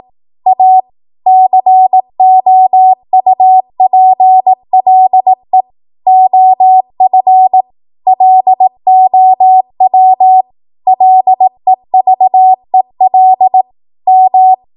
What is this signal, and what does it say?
Morse code, heard as a keyed tone, about 18 wpm, 750 Hz. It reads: A COUPLE OF LOW LEVEL M
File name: Archive / 18 WPM / 18WPM